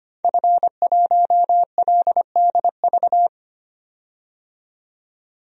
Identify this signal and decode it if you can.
Morse code, heard as a keyed tone, about 25 wpm, 700 Hz. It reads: F1LDV
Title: DXCC-100 Call Signs - 5 Characters - 25wpm